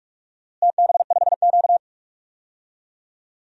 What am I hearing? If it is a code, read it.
Morse code, heard as a keyed tone, about 46 wpm, 700 Hz. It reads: TB5Q
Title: DXCC-100 Call Signs - 4 Characters - Repeated 3x - 45wpm